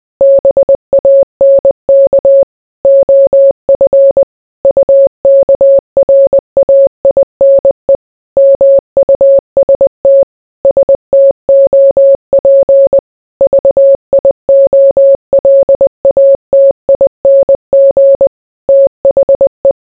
Morse code, heard as a keyed tone, about 20 wpm, 550 Hz. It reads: BANK OF UKRAINE MUST STOP VIOLATING THE